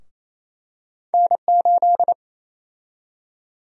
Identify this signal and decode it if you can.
Morse code, heard as a keyed tone, about 27 wpm, 700 Hz. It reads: N8
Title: USA Call Sign Prefixes - 28wpm